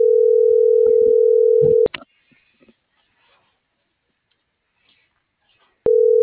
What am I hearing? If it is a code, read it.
Ringback tone (North America), 440 + 480 Hz.